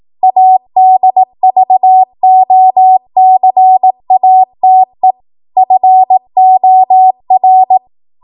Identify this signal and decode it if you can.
Morse code, heard as a keyed tone, about 18 wpm, 750 Hz. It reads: ADVOCATE FOR
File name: Morse / 18WPM